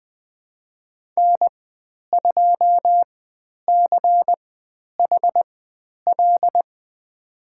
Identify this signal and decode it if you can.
Morse code, heard as a keyed tone, about 20 wpm, spaced as for 10 wpm, 700 Hz. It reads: N2CHL
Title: 095 Lesson 16g - Letter c - Call Signs 20wpm